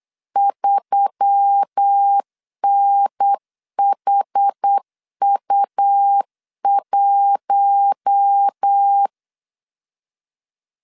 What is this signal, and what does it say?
Morse code, heard as a keyed tone, about 8 wpm, 800 Hz. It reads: 3NHU1
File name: LEZIONI / 6 PKX0 / 6 PKX 8wpm 3041